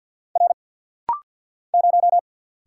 Morse code, heard as a keyed tone, about 50 wpm, 700 Hz. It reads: R 0
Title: Single Letter-Number Alt - Rapid-Fire 50wpm